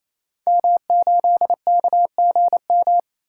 Morse code, heard as a keyed tone, about 27 wpm, 700 Hz. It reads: M8KGM
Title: Code Groups - 5 Letter:Numbers - 28wpm